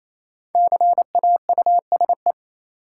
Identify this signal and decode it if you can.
Morse code, heard as a keyed tone, about 28 wpm, 700 Hz. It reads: CAUSE
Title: Sentences from Top 200 Words - New Word Drill - 28wpm